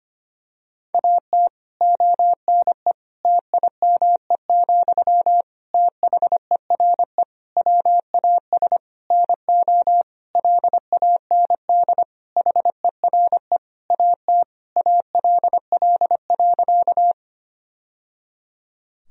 Morse code, heard as a keyed tone, about 25 wpm, 700 Hz. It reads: AT ONE TIME, THERE WAS NO LAND HERE AT ALL.
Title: Sentences from Top 1000 Words - Review Part 1 - 25wpm